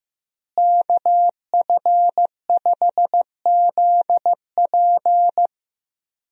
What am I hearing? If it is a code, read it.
Morse code, heard as a keyed tone, about 16 wpm, 700 Hz. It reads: KF5ZP
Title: USA Call Signs - 5 Characters - Repeated 3x - 15wpm